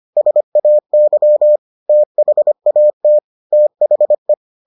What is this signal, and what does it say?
Morse code, heard as a keyed tone, about 25 wpm, 600 Hz. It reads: SAY THAT THE